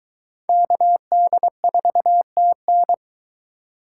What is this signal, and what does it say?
Morse code, heard as a keyed tone, about 21 wpm, 700 Hz. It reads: KD4TN